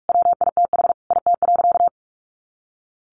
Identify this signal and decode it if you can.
Morse code, heard as a keyed tone, about 44 wpm, 700 Hz. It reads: WITH IT.
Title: A Sound Of Thunder 45wpm